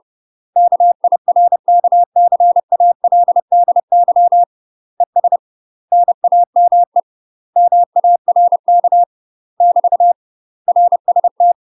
Morse code, heard as a keyed tone, about 30 wpm, 700 Hz. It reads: KIRKCALDY ES NAME MARK <BT> RST
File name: UK MASTER QSOs TRAINING / 30WPM / UK MASTER QSOs 0051